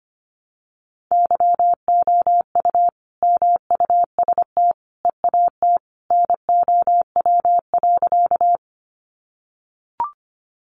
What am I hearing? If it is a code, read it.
Morse code, heard as a keyed tone, about 25 wpm, 700 Hz. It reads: YOU MUST EAT NOW.
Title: Sentences from Top 300 Words 25wpm